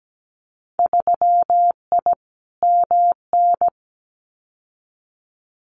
Morse code, heard as a keyed tone, about 17 wpm, 700 Hz. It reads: 3I MN
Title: Winter Field Day Exchanges 17wpm